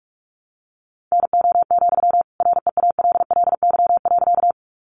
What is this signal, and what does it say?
Morse code, heard as a keyed tone, about 44 wpm, 700 Hz. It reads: NO, REALLY.
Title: Sentences from Top 600 Words 45wpm